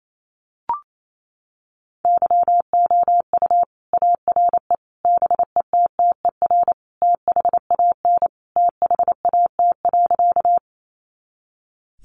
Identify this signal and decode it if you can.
Morse code, heard as a keyed tone, about 28 wpm, 700 Hz. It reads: YOU ARE BETTER THAN THAT.
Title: Sentences from Top 400 Words 28wpm